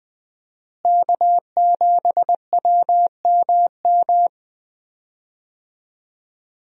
Morse code, heard as a keyed tone, about 20 wpm, 700 Hz. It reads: K7WMM
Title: USA Call Signs - 5 Characters - Repeated 3x - 20wpm